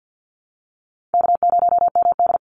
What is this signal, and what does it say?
Morse code, heard as a keyed tone, about 48 wpm, 700 Hz. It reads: K0MD
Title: Call Signs Repeated 3x - 50wpm